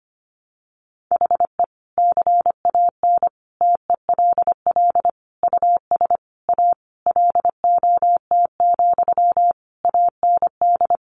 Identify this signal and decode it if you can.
Morse code, heard as a keyed tone, about 25 wpm, 700 Hz. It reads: HE CAN TELL US A LOT, AND
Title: Sentences from Top 1300 Words 25wpm